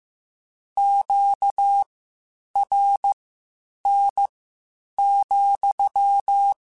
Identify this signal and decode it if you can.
Morse code, heard as a keyed tone, about 15 wpm, spaced as for 8 wpm, 800 Hz. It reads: QRN,
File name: Q Calls / Q Calls-mp3s / Q Calls sppc @13